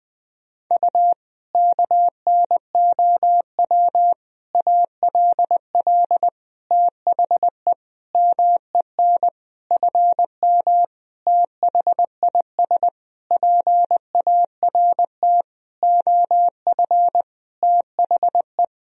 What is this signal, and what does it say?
Morse code, heard as a keyed tone, about 20 wpm, 700 Hz. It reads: U KNOW ALL THE MEN FM THIS PART OF THE
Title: Sentences from Top 300 Words and Abbreviations 20wpm